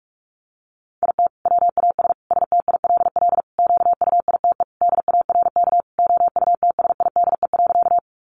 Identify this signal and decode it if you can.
Morse code, heard as a keyed tone, about 44 wpm, 700 Hz. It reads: IT WAS STILL QUITE DARK OUTSIDE.
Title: Sentences from Top 900 Words 45wpm